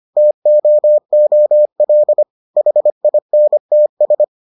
Morse code, heard as a keyed tone, about 25 wpm, 600 Hz. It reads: TOOL HINTS